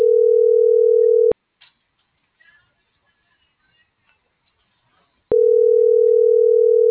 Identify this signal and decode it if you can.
Ringback tone (North America), 440 + 480 Hz.
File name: CENTRAL TERRESTRE / out vane.lazaro